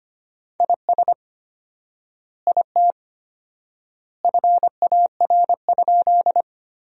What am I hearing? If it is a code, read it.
Morse code, heard as a keyed tone, about 25 wpm, 700 Hz. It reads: IS IT FAR?
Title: 4x Word-Spacing - Sentences from Top 300 Words - 25wpm